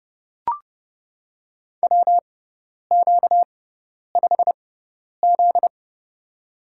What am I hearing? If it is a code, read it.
Morse code, heard as a keyed tone, about 30 wpm, spaced as for 10 wpm, 690 Hz. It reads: WQ5Z
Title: USA Call Signs - 4 Characters - 6x Character Spacing - 30wpm